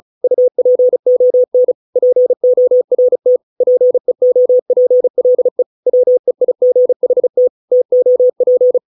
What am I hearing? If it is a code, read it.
Morse code, heard as a keyed tone, about 34 wpm, 500 Hz. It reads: UPON PORT PEOPLE WEIGHT TOP